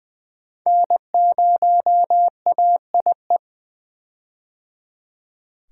Morse code, heard as a keyed tone, about 20 wpm, 700 Hz. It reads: N0AIE
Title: K1USN SST QSO Practice - 20wpm